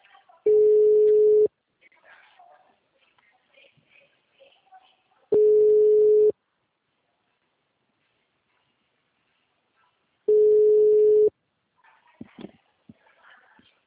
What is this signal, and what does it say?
Ringback tone (Europe, 425 Hz), with speech.